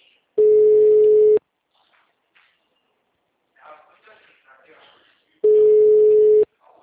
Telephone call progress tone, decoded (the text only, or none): ringback tone